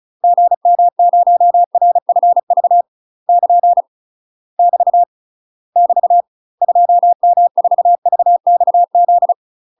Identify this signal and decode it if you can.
Morse code, heard as a keyed tone, about 36 wpm, 700 Hz. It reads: GM0RFV <KN> <BT> <BT> 2M4VXZ
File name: UK MASTER QSOs TRAINING / 35WPM / UK MASTER QSOs 0009